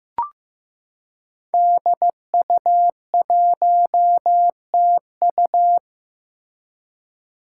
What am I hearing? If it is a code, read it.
Morse code, heard as a keyed tone, about 15 wpm, 700 Hz. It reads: DU1TU
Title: Five Challenging Letter-Number Combinations 15wpm